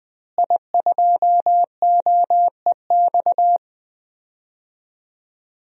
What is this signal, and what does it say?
Morse code, heard as a keyed tone, about 20 wpm, 700 Hz. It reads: I2OEX
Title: DXCC-100 Call Signs - 5 Characters - Repeated 2x - 20wpm